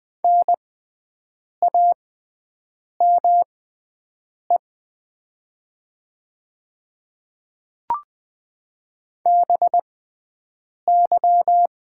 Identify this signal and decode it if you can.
Morse code, heard as a keyed tone, about 20 wpm, spaced as for 7 wpm, 700 Hz. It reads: NAME BY
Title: 6x Character Spacing - Top 200 Words - 20wpm